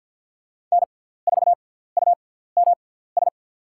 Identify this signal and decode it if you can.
Morse code, heard as a keyed tone, about 48 wpm, 700 Hz. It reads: N 4 U M S
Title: USA Call Signs - 5 Characters - 6x Character Spacing - 50wpm